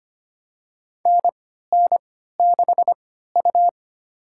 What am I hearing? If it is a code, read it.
Morse code, heard as a keyed tone, about 25 wpm, spaced as for 14 wpm, 700 Hz. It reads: NN6U
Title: USA Call Signs - 4 Characters - 3x Character Spacing - 25wpm